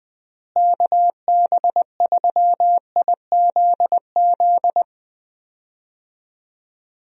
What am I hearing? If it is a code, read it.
Morse code, heard as a keyed tone, about 20 wpm, 700 Hz. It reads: KB3IZZ
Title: USA Call Signs - 6 Characters - Repeated 2x - 20wpm